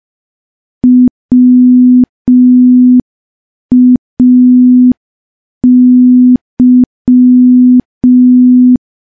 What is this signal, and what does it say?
Morse code, heard as a keyed tone, about 5 wpm, 250 Hz. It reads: WAY
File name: morse / morse-training-05WPM